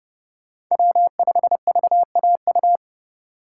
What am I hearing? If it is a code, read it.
Morse code, heard as a keyed tone, about 30 wpm, 700 Hz. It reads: W5VAU
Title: USA Call Signs - 5 Characters - 4x Character Spacing - 30wpm